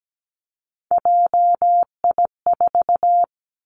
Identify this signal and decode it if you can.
Morse code, heard as a keyed tone, about 17 wpm, 700 Hz. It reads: JI4
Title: Three Challenging Letter-Number Combinations 17wpm